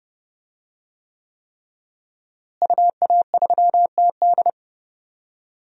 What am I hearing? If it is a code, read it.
Morse code, heard as a keyed tone, about 30 wpm, 700 Hz. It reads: UA3TD
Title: DXCC-100 Call Signs - 5 Characters - Repeated 3x - 30wpm